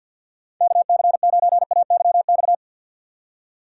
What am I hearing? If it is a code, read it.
Morse code, heard as a keyed tone, about 48 wpm, 700 Hz. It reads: KC9AYX
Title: USA Call Signs - 6 Characters - 4x Character Spacing - 50wpm